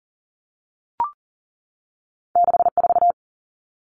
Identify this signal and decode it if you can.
Morse code, heard as a keyed tone, about 40 wpm, 700 Hz. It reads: 64
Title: Common Cut Numbers - 2-Digit 40wpm